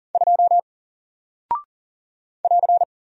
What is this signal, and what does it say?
Morse code, heard as a keyed tone, about 40 wpm, 700 Hz. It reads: 2 <AR>
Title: Single Character (Letter:Number:Punctuation:Prosign) Alt 40wpm